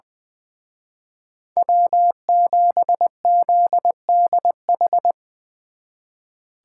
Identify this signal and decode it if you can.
Morse code, heard as a keyed tone, about 22 wpm, 700 Hz. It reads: W7ZDH
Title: Speed Racing - USA Call Signs 20wpm